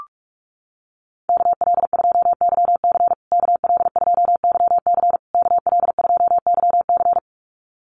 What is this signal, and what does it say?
Morse code, heard as a keyed tone, about 44 wpm, 690 Hz. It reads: KL2YC KL2YC KL2YC
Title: USA Call Signs Repeated 3x 45wpm